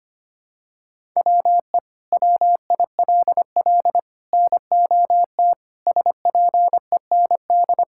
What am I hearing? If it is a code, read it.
Morse code, heard as a keyed tone, about 25 wpm, 700 Hz. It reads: WE WILL NOT SPEND
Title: Sentences from Top 1000 Words - Review Part 5 - 25wpm